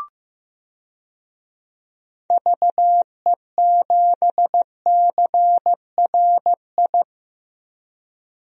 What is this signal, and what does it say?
Morse code, heard as a keyed tone, about 15 wpm, 700 Hz. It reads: VE7CRI
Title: DXCC-100 Call Signs - 6 Characters - Repeated 3x - 15wpm